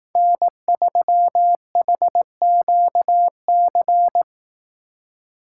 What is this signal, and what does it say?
Morse code, heard as a keyed tone, about 18 wpm, 700 Hz. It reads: N3HQC
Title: Speed Racing - USA 1x3 Call Signs 15wpm